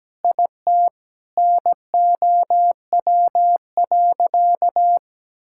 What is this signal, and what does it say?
Morse code, heard as a keyed tone, about 17 wpm, 700 Hz. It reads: IT NOW.
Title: Sentences from Top 1000 Words - Review Part 6 - 17wpm